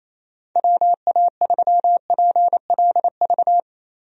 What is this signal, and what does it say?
Morse code, heard as a keyed tone, about 28 wpm, 700 Hz. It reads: WA3PLV
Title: USA Call Signs - 6 Characters - Repeated 2x - 28wpm